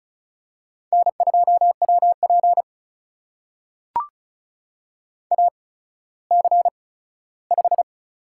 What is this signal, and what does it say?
Morse code, heard as a keyed tone, about 36 wpm, spaced as for 17 wpm, 700 Hz. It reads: N2WP AC5
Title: USA Call Signs - 4 Characters - 8x Character Spacing - 35wpm